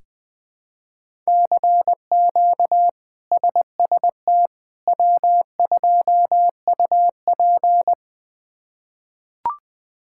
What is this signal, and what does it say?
Morse code, heard as a keyed tone, about 20 wpm, 700 Hz. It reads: CQ SST W2UP
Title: K1USN SST QSO Practice - 20wpm